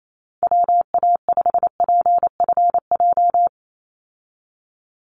Morse code, heard as a keyed tone, about 28 wpm, 700 Hz. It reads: WA5PFJ
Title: Call Signs - 28wpm